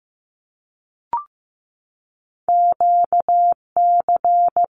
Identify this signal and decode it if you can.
Morse code, heard as a keyed tone, about 15 wpm, 700 Hz. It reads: QC
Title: Canadian Provinces and Territories 15wpm